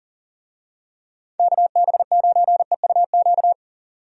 Morse code, heard as a keyed tone, about 40 wpm, 700 Hz. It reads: KB9EUQ
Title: USA Call Signs - 6 Characters - 40wpm